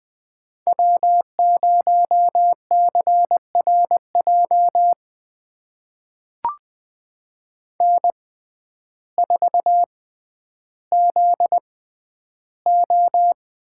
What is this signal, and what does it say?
Morse code, heard as a keyed tone, about 20 wpm, spaced as for 11 wpm, 700 Hz. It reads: W0CRJ N4ZO
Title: USA Call Signs - 5 Characters - 6x Character Spacing - 20wpm